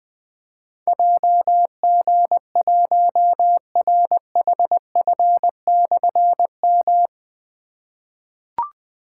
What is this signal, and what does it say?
Morse code, heard as a keyed tone, about 20 wpm, 700 Hz. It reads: JG1RHF/M
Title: Difficult Call Signs 20wpm